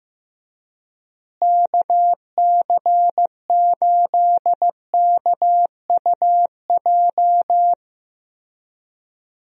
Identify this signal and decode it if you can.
Morse code, heard as a keyed tone, about 16 wpm, 700 Hz. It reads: KC8KUJ
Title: USA Call Signs - 6 Characters - Repeated 3x - 15wpm